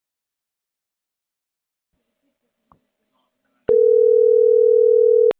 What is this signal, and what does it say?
Ringback tone (North America), 440 + 480 Hz.